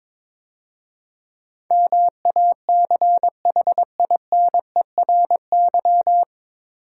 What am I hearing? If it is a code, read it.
Morse code, heard as a keyed tone, about 23 wpm, 690 Hz. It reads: MACHINERY
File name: 9 Letter Words - First 3 Letters Given - 22wpm